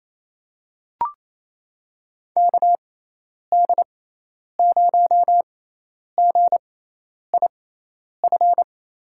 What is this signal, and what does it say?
Morse code, heard as a keyed tone, about 28 wpm, spaced as for 10 wpm, 700 Hz. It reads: KD0GIF
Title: USA Call Signs - 6 Characters - 6x Character Spacing - 28wpm